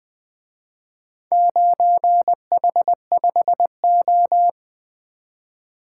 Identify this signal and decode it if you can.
Morse code, heard as a keyed tone, about 23 wpm, 700 Hz. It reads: 9H5O
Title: Speed-Racing - Four Letter-Number Combinations - 20wpm